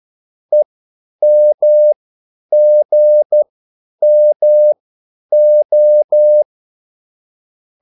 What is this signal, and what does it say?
Morse code, heard as a keyed tone, about 12 wpm, spaced as for 9 wpm, 600 Hz. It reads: EMGMO
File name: QSL / CW lesson